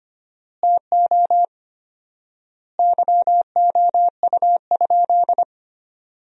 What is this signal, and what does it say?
Morse code, heard as a keyed tone, about 25 wpm, 700 Hz. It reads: TO YOU?
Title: 4x Word-Spacing - Sentences from Top 100 Words - 25wpm